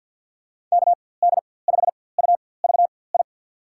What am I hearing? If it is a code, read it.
Morse code, heard as a keyed tone, about 47 wpm, spaced as for 23 wpm, 700 Hz. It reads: KD5UVI
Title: USA Call Signs - 6 Characters - 4x Character Spacing - 50wpm